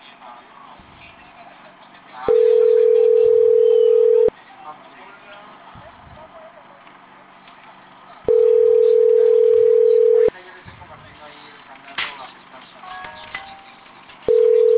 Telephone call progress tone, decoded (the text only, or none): ringback tone